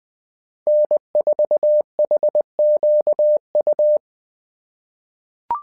Morse code, heard as a keyed tone, about 20 wpm, 600 Hz. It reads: N4HQU
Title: SOTA QSO Practice 20wpm